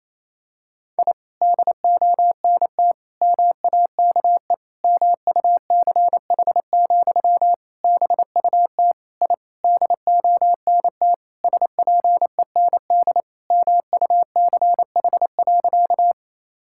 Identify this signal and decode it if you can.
Morse code, heard as a keyed tone, about 28 wpm, 700 Hz. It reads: I DONT MAKE MUCH, BUT I DONT SPEND MUCH.
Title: Sentences from Top 1000 Words - Review Part 5 - 28wpm